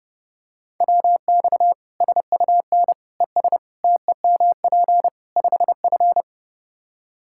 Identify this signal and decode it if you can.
Morse code, heard as a keyed tone, about 30 wpm, 700 Hz. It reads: WX SUN ES TEMP 5F
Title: Practice QSO - Exchange 2 - Info - 30wpm